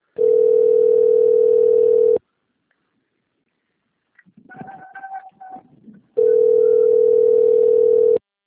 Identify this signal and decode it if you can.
Ringback tone (North America), 440 + 480 Hz.
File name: rg-1600-unknown- - - .85394